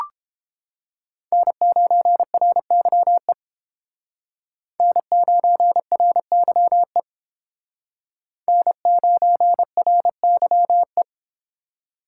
Morse code, heard as a keyed tone, about 29 wpm, 700 Hz. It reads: N9RYE N9RYE N9RYE